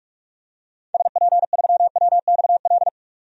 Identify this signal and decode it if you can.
Morse code, heard as a keyed tone, about 47 wpm, 700 Hz. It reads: SP3WXL
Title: DXCC-100 Call Signs - 6 Characters - 45wpm